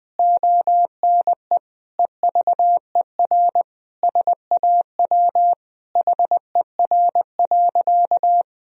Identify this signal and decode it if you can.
Morse code, heard as a keyed tone, about 21 wpm, 700 Hz. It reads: ONE EVER SAW HER.
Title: Sentences from Top 1000 Words - Review Part 2 - 20wpm